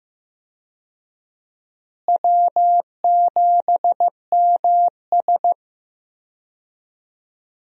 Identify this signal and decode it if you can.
Morse code, heard as a keyed tone, about 15 wpm, 700 Hz. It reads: W7MS
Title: USA Call Signs - 4 Characters - Repeated 3x - 15wpm